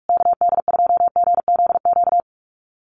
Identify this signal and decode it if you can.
Morse code, heard as a keyed tone, about 44 wpm, 700 Hz. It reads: KD2GZQ
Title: USA Call Signs Repeated 3x 45wpm